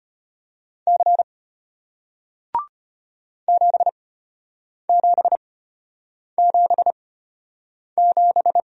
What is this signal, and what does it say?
Morse code, heard as a keyed tone, about 34 wpm, spaced as for 8 wpm, 700 Hz. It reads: C 7777